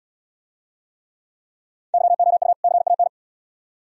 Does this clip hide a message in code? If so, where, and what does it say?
Yes: Morse code from 1.9 to 3.1 s, heard as a keyed tone, about 74 wpm, 700 Hz. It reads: YOU CAN